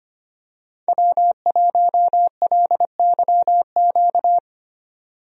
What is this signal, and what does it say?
Morse code, heard as a keyed tone, about 25 wpm, 690 Hz. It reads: W1LYQ
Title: USA Call Signs - 5 Characters - 25wpm